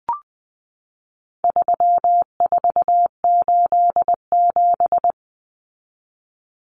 Morse code, heard as a keyed tone, about 20 wpm, 700 Hz. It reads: 3487
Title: Numbers - 4-Digit 20wpm